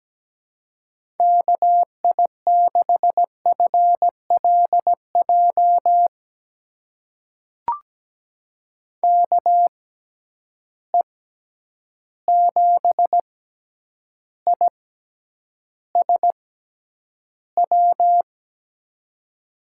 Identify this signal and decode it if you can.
Morse code, heard as a keyed tone, about 17 wpm, spaced as for 9 wpm, 700 Hz. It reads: KI6FLJ KE7ISW
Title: USA Call Signs - 6x Character Spacing - 17wpm